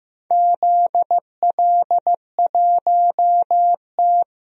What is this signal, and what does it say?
Morse code, heard as a keyed tone, about 16 wpm, 700 Hz. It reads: ZL1T
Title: DXCC-100 Call Signs - 4 Characters - Repeated 2x - 15wpm